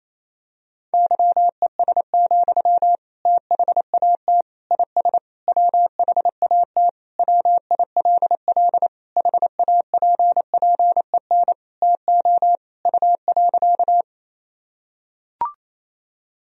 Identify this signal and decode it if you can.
Morse code, heard as a keyed tone, about 28 wpm, 700 Hz. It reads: YES, THAT IS WHAT WILL HAPPEN TO U.
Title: Sentences from Top 500 Words and Abbreviations 28wpm